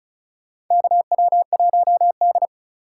Morse code, heard as a keyed tone, about 36 wpm, 700 Hz. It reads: KW1D
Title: USA Call Signs - 4x Character Spacing - 35wpm